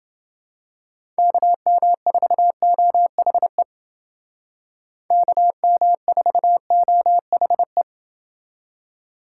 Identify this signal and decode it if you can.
Morse code, heard as a keyed tone, about 28 wpm, 700 Hz. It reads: KM4OHE KM4OHE